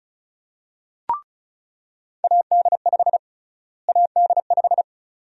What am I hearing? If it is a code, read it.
Morse code, heard as a keyed tone, about 36 wpm, 700 Hz. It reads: AD5 AD5
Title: USA Call Sign Prefixes - Repeated 2x - 35wpm